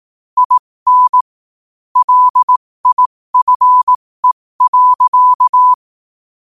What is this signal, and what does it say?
Morse code, heard as a keyed tone, about 18 wpm, spaced as for 15 wpm, 1000 Hz. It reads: IN LIFE.
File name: qotdfiles / 15wpm